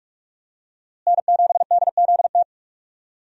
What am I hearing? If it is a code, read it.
Morse code, heard as a keyed tone, about 46 wpm, 700 Hz. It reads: N7DZT
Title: USA Call Signs - 5 Characters - Repeated 2x - 45wpm